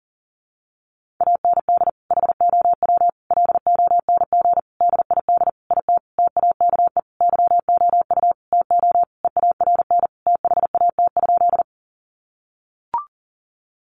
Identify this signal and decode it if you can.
Morse code, heard as a keyed tone, about 40 wpm, 700 Hz. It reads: AND HOW LONG DID IT TAKE YOU TO EARN THAT?
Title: Sentences from Top 1500 Words 40wpm